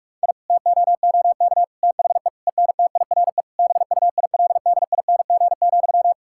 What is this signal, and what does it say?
Morse code, heard as a keyed tone, about 41 wpm, 700 Hz. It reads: I TOOK THE ENTIRE BUILDING,
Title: Sentences from Top 1000 Words - Review Part 5 - 45wpm